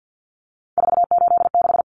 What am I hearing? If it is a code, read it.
Morse code, heard as a keyed tone, about 50 wpm, 700 Hz. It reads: 486
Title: Common Cut Numbers - 3-Digit 50wpm